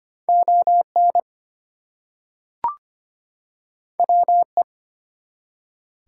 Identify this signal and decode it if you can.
Morse code, heard as a keyed tone, about 26 wpm, 700 Hz. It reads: ON WE
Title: Two Letter Words - 25wpm